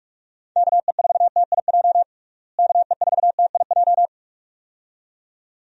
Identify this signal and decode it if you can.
Morse code, heard as a keyed tone, about 46 wpm, 700 Hz. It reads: KE4TIJ KE4TIJ
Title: USA Call Signs - 6 Characters - Repeated 2x - 45wpm